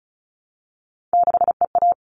Morse code, heard as a keyed tone, about 34 wpm, 700 Hz. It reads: 6EA
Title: All Cut Numbers - 3-Digit 35wpm